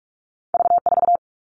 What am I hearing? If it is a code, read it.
Morse code, heard as a keyed tone, about 46 wpm, 700 Hz. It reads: V4
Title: All Cut Numbers - 2-Digit 45wpm